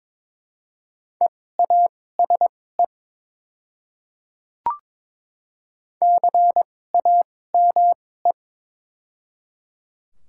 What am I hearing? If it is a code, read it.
Morse code, heard as a keyed tone, about 22 wpm, spaced as for 16 wpm, 700 Hz. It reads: EASE CAME
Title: 2x Character Spacing - Top 300 Words - 22wpm